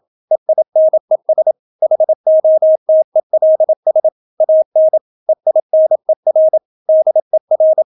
Morse code, heard as a keyed tone, about 27 wpm, 630 Hz. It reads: EINES HOTELS AN EINER DER